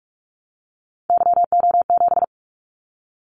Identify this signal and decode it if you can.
Morse code, heard as a keyed tone, about 46 wpm, 700 Hz. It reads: YO7
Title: Callsign Prefixes for Top 25 DXCC - 45wpm